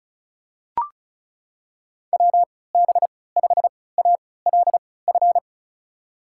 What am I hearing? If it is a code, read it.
Morse code, heard as a keyed tone, about 36 wpm, spaced as for 20 wpm, 700 Hz. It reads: WB5ALF
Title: USA Call Signs - 6 Characters - 3x Character Spacing - 35wpm